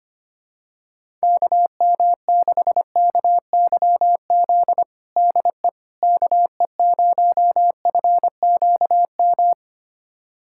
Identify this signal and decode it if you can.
Morse code, heard as a keyed tone, about 25 wpm, 700 Hz. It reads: KM6KYZ DE KE0FQM
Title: USA Callsign DE Callsign 25wpm